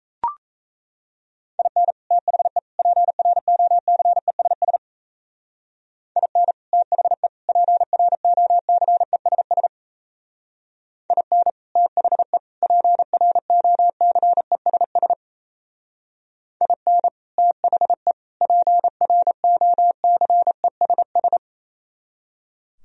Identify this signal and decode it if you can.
Morse code, heard as a keyed tone, about 35 wpm, 700 Hz. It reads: IN THE PROCESS IN THE PROCESS IN THE PROCESS IN THE PROCESS
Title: Speed-Racing - Sets of 3 Words 28wpm